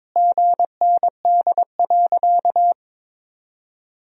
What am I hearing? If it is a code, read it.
Morse code, heard as a keyed tone, about 22 wpm, 700 Hz. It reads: GND.
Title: Sentences from Top 500 Words and Abbreviations 22wpm